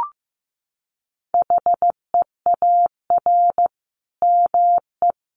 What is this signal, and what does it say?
Morse code, heard as a keyed tone, about 15 wpm, 700 Hz. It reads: HEAR ME
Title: Sentences from Top 1400 Words 15wpm